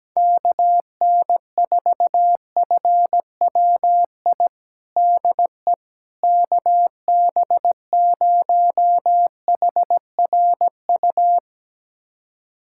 Morse code, heard as a keyed tone, about 17 wpm, 700 Hz. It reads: KN4FWI DE KB0HRU
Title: USA Callsign DE Callsign - 6 Characters - 17wpm